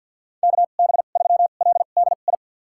Morse code, heard as a keyed tone, about 47 wpm, spaced as for 35 wpm, 700 Hz. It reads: KB3LDI
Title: USA Call Signs - 2x Character Spacing - 50wpm